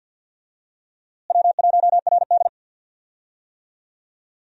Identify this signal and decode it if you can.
Morse code, heard as a keyed tone, about 53 wpm, 700 Hz. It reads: W1RD
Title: USA Call Signs - 4 Characters - Repeated 2x - 50wpm